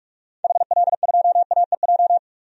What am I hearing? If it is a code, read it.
Morse code, heard as a keyed tone, about 41 wpm, 700 Hz. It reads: HL2AEJ
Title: DXCC-100 Call Signs - 6 Characters - 45wpm